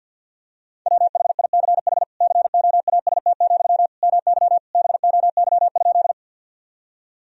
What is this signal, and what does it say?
Morse code, heard as a keyed tone, about 50 wpm, 700 Hz. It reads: WHICH COAST, MY BOY?
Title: Sentences from Top 1000 Words - Review Part 4 - 50wpm